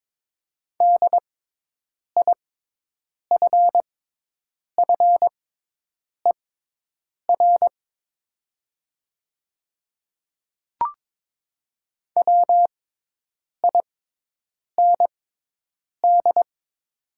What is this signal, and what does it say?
Morse code, heard as a keyed tone, about 22 wpm, spaced as for 8 wpm, 700 Hz. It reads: DIFFER WIND
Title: 6x Character Spacing - Top 500 Words - 22wpm